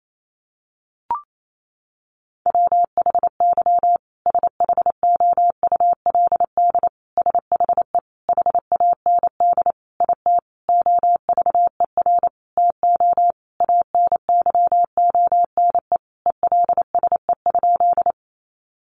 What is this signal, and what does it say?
Morse code, heard as a keyed tone, about 28 wpm, 700 Hz. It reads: WHY SHOULD SHE HAND IT OVER TO ANYONE ELSE?
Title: Sentences from Top 900 Words 28wpm